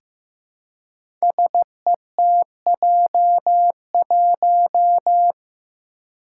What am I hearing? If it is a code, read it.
Morse code, heard as a keyed tone, about 15 wpm, 700 Hz. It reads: SETJ1
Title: Five Challenging Letter-Number Combinations 15wpm